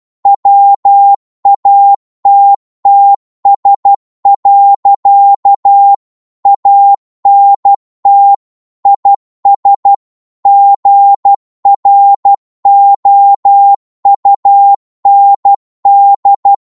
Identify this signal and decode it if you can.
Morse code, heard as a keyed tone, about 12 wpm, 800 Hz. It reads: WATTS. ANT IS GROUND
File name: Morse / qso-12wpm